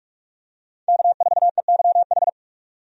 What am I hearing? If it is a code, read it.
Morse code, heard as a keyed tone, about 46 wpm, 700 Hz. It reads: K4EYH
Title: USA Call Signs - 5 Characters - 45wpm